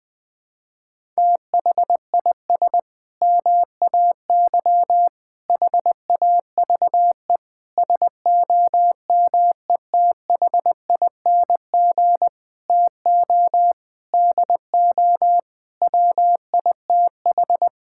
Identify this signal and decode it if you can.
Morse code, heard as a keyed tone, about 20 wpm, 700 Hz. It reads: THIS MAY HAVE SOMETHING TO DO WITH
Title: Sentences with 8-Word Phrases - 20wpm